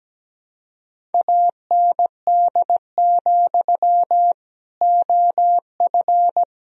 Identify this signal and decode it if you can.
Morse code, heard as a keyed tone, about 17 wpm, 700 Hz. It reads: AND, OF
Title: Sentences from Top 1000 Words - Review Part 1 - 17wpm